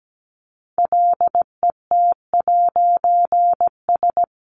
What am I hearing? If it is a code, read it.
Morse code, heard as a keyed tone, about 17 wpm, 700 Hz. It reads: LET'S
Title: A Sound Of Thunder 17wpm